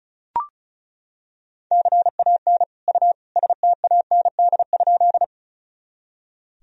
Morse code, heard as a keyed tone, about 36 wpm, 700 Hz. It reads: CAN U STAND?
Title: Sentences from Top 300 Words and Abbreviations 35wpm